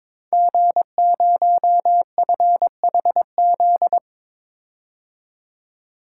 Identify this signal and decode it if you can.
Morse code, heard as a keyed tone, about 23 wpm, 700 Hz. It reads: G0FHZ
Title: Code Groups - 5 Letter:Numbers - 22wpm